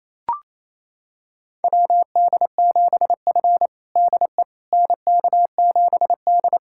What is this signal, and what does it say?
Morse code, heard as a keyed tone, about 28 wpm, 700 Hz. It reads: WD7F DE NK7D
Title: USA Callsign DE Callsign - 4 Characters - 28wpm